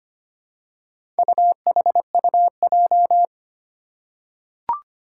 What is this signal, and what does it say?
Morse code, heard as a keyed tone, about 25 wpm, 700 Hz. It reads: UHUJ
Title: Four Challenging Letter-Number Combinations 25wpm